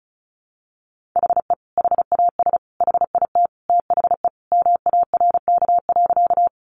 Morse code, heard as a keyed tone, about 34 wpm, 700 Hz. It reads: HE HAS HIT THE MARK.
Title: Sentences from Top 1200 Words 35wpm